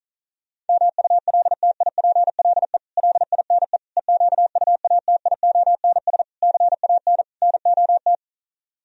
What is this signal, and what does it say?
Morse code, heard as a keyed tone, about 46 wpm, 700 Hz. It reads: MULTIPLE LINE EQUATIONS CAN NOT